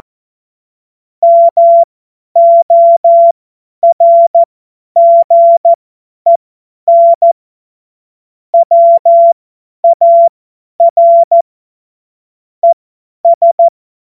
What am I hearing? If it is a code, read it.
Morse code, heard as a keyed tone, about 14 wpm, spaced as for 10 wpm, 680 Hz. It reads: MORGEN WAR ES
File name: Morsekurs / Witze / Witz 22. August 2023 50 F70